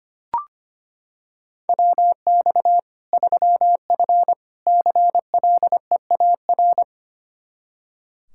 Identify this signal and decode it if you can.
Morse code, heard as a keyed tone, about 25 wpm, 700 Hz. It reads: WX 3F CLEAR
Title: K1USN SST QSO Practice 25wpm